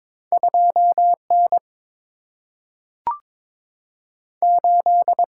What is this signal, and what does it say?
Morse code, heard as a keyed tone, about 22 wpm, 700 Hz. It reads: 2 N 8
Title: Single-Letter Single-Number 22wpm